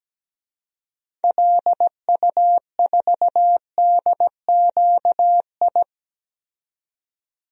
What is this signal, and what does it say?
Morse code, heard as a keyed tone, about 17 wpm, 700 Hz. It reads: LU4DQI
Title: DXCC-100 Call Signs - 6 Characters - 17wpm